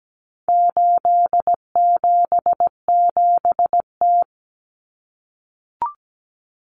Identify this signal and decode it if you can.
Morse code, heard as a keyed tone, about 17 wpm, 700 Hz. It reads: 877T
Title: Common Cut Numbers - 4-Digit 17wpm